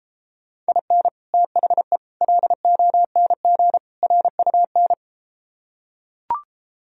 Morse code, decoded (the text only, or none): IN THE LONG RUN